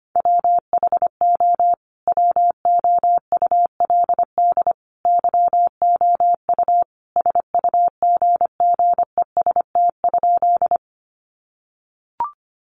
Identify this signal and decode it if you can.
Morse code, heard as a keyed tone, about 25 wpm, 700 Hz. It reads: WHO WOULD YOU SUGGEST?
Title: Sentences from Top 1300 Words 25wpm